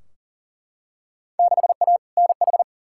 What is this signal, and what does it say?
Morse code, heard as a keyed tone, about 40 wpm, 700 Hz. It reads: 6A NH
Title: Field Day Exchanges - 40wpm